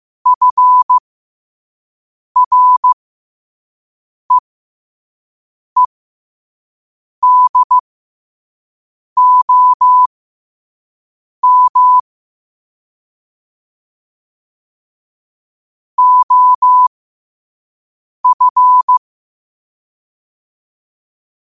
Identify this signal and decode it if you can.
Morse code, heard as a keyed tone, about 15 wpm, spaced as for 5 wpm, 1000 Hz. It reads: FREEDOM OF
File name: qotdfiles / 5wpm